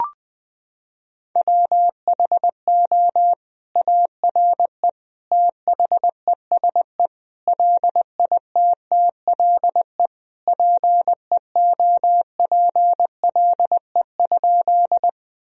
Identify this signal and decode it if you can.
Morse code, heard as a keyed tone, about 20 wpm, 700 Hz. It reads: WHO ARE THESE LITTLE PEOPLE?
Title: Sentences from Top 1000 Words - Review Part 1 - 20wpm